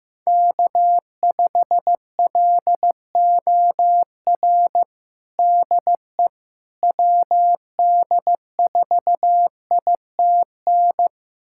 Morse code, heard as a keyed tone, about 15 wpm, 700 Hz. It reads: K5LOR DE WD4ITN
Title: USA Callsign DE Callsign 15wpm